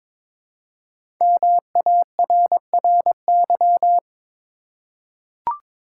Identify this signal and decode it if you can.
Morse code, heard as a keyed tone, about 21 wpm, 700 Hz. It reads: MARRY
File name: 5 Letter Words - First 3 Letters Given - 22wpm